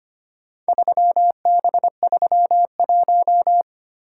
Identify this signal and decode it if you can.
Morse code, heard as a keyed tone, about 25 wpm, 690 Hz. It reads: 3B31
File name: practice / 25wpm / Four Letter-Number Combinations - 25wpm